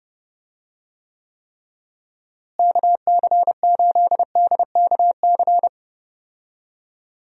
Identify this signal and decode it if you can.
Morse code, heard as a keyed tone, about 30 wpm, 700 Hz. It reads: KC8DKC
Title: USA Call Signs - 6 Characters - Repeated 3x - 30wpm